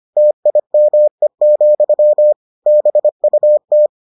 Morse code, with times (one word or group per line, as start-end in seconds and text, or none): 0.2-2.3 TIME,
2.7-3.9 BUT